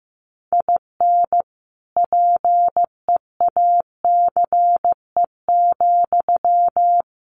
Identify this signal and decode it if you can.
Morse code, heard as a keyed tone, about 15 wpm, 700 Hz. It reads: IN PEACE,
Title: Sentences from Top 1500 Words 15wpm